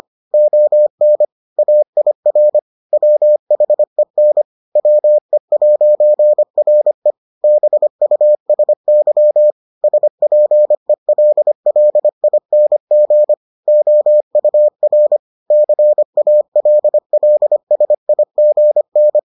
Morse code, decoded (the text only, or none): ON AIR WHEN WE'RE BUSY SPELLING OUR CALLSIGN